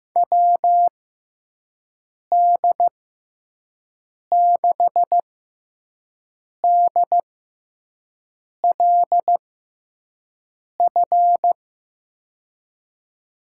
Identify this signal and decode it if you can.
Morse code, heard as a keyed tone, about 15 wpm, spaced as for 5 wpm, 700 Hz. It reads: WD6DLF
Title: USA Call Signs - 6 Characters - 6x Character Spacing - 15wpm